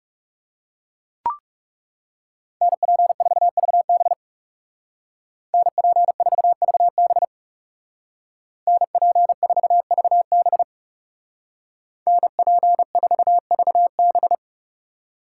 Morse code, heard as a keyed tone, about 36 wpm, 700 Hz. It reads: NP4VB NP4VB NP4VB NP4VB
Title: Speed Racing - USA 2x2 Call Signs 30wpm